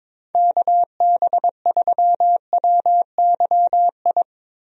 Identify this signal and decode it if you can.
Morse code, heard as a keyed tone, about 22 wpm, 700 Hz. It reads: KB3WYI
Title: USA Call Signs - 6 Characters - Repeated 3x - 22wpm